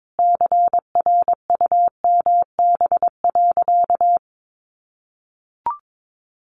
Morse code, heard as a keyed tone, about 22 wpm, 700 Hz. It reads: CRUMB.
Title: Additional Easy Phrases 22wpm